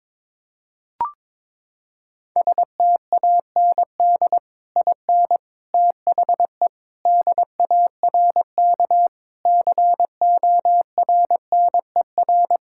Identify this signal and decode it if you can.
Morse code, heard as a keyed tone, about 22 wpm, 700 Hz. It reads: STAND IN THE DARK CORNER
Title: Sentences from Top 1600 Words 22wpm